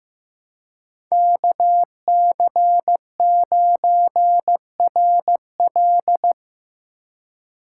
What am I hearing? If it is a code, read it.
Morse code, heard as a keyed tone, about 18 wpm, 700 Hz. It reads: KC9RL
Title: Speed Racing - USA 2x2 Call Signs 15wpm